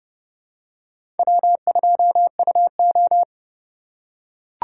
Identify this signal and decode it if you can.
Morse code, heard as a keyed tone, about 30 wpm, 700 Hz. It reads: W2UO
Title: USA Call Signs - 4 Characters - 3x Character Spacing - 30wpm